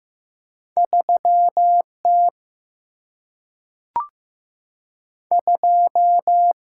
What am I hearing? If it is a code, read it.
Morse code, heard as a keyed tone, about 15 wpm, 700 Hz. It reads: 3 T 2
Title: Single-Letter Single-Number 15wpm